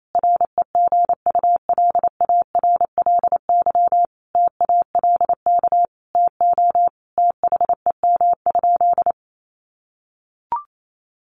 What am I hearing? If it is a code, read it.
Morse code, heard as a keyed tone, about 28 wpm, 700 Hz. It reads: REGULARLY TALK TO THEM?
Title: Sentences from Top 1400 Words 28wpm